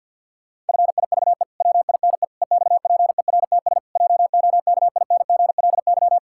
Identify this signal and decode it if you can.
Morse code, heard as a keyed tone, about 52 wpm, 700 Hz. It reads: FIVE WINE EXPERTS JOKINGLY
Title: Pangram Challenge v2 - 50wpm